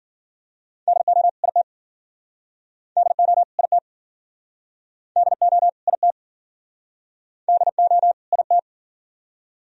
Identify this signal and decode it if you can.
Morse code, heard as a keyed tone, about 58 wpm, 700 Hz. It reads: DO IT DO IT DO IT DO IT